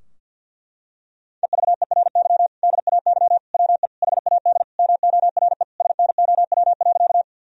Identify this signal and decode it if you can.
Morse code, heard as a keyed tone, about 50 wpm, 700 Hz. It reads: EVERY DAY WE HAD MORE SNOW.
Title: Sentences from Top 1000 Words - Review Part 3 - 50wpm